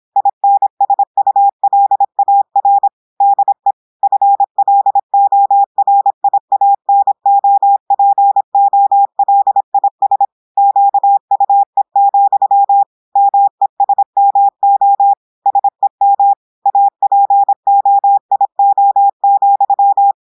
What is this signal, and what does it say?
Morse code, heard as a keyed tone, about 26 wpm, 800 Hz. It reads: INSULAR DE FLORIANOPOLIS QUE, MESMO SEM APOIO,
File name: audios / portugais / portugais texte en clair 1 vitesse 26 mots minutes